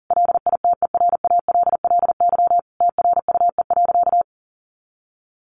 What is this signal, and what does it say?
Morse code, heard as a keyed tone, about 40 wpm, 700 Hz. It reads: LITERALLY TRUE.
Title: Sentences from Top 1400 Words 40wpm